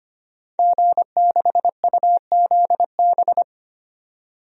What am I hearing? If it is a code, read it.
Morse code, heard as a keyed tone, about 25 wpm, 700 Hz. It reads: G6UZB
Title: DXCC-100 Call Signs - 5 Characters - 25wpm